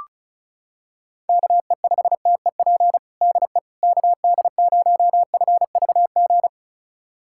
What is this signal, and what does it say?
Morse code, heard as a keyed tone, about 36 wpm, 700 Hz. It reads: KE5TEP DE KD0FVG
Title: USA Callsign DE Callsign - 6 Characters - 35wpm